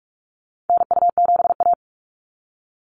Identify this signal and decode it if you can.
Morse code, heard as a keyed tone, about 46 wpm, 700 Hz. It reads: NU7A
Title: NA SOTA Chaser Callsigns 45wpm